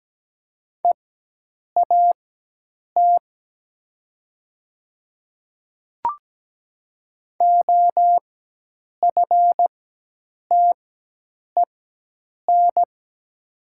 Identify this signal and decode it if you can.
Morse code, heard as a keyed tone, about 17 wpm, spaced as for 8 wpm, 700 Hz. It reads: EAT OFTEN
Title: 4x Character Spacing - Top 500 Words - 17wpm